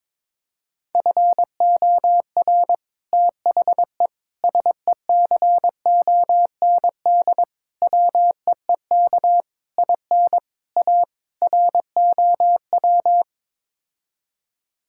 Morse code, heard as a keyed tone, about 22 wpm, 700 Hz. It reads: FOR THE SECOND WEEK IN A ROW
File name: practice / 22wpm / Sets of 7 Words - 22wpm